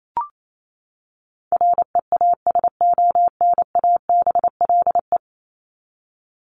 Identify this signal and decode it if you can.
Morse code, heard as a keyed tone, about 28 wpm, 700 Hz. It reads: REASONABLE
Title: Top 1501-1750 Words 28wpm